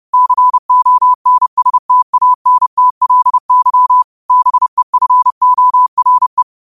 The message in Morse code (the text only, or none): CONSTANTLY BEFORE